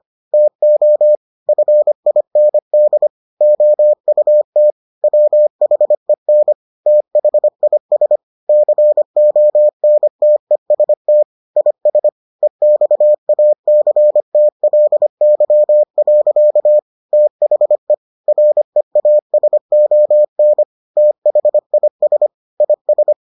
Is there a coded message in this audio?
Yes: Morse code from 0.3 to 23.1 s, heard as a keyed tone, about 25 wpm, 600 Hz. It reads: TO FIND OUT WHEN THIS CONTEST IS EXACTLY. THE REASON THIS IS